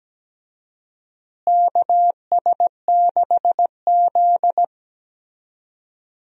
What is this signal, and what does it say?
Morse code, heard as a keyed tone, about 17 wpm, 700 Hz. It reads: KS6Z
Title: USA Call Signs - 4 Characters - Repeated 2x - 17wpm